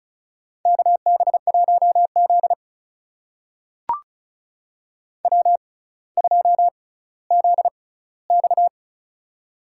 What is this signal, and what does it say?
Morse code, heard as a keyed tone, about 36 wpm, spaced as for 18 wpm, 700 Hz. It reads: KB1Z W2ZX
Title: USA Call Signs - 4 Characters - 6x Character Spacing - 35wpm